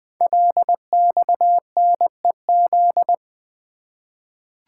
Morse code, heard as a keyed tone, about 20 wpm, 700 Hz. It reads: LXNEZ
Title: Code Groups - 5 Letters - 20wpm